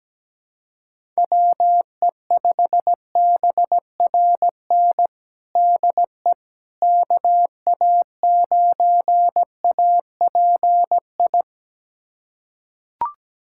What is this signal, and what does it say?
Morse code, heard as a keyed tone, about 17 wpm, 700 Hz. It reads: WE5BRN DE KA9API
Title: USA Callsign DE Callsign - 6 Characters - 17wpm